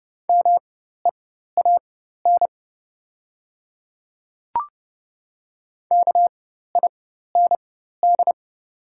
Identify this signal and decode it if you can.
Morse code, heard as a keyed tone, about 30 wpm, spaced as for 14 wpm, 700 Hz. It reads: MEAN KIND
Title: 4x Character Spacing - Top 300 Words - 30wpm